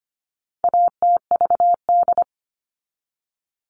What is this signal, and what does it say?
Morse code, heard as a keyed tone, about 25 wpm, 700 Hz. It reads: ATVD
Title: All Cut Numbers - 4-Digit 25wpm